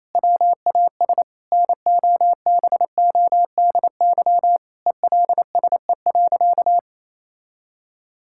Morse code, heard as a keyed tone, about 28 wpm, 700 Hz. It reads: WAS NOBODY ELSE.
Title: Sentences from Top 1600 Words 28wpm